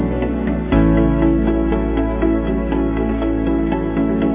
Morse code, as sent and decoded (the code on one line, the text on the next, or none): none